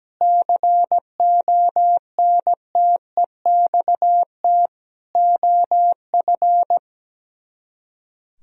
Morse code, heard as a keyed tone, about 17 wpm, 700 Hz. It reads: CONTEXT OF
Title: Speed-Racing - Sets of 4 Words 17wpm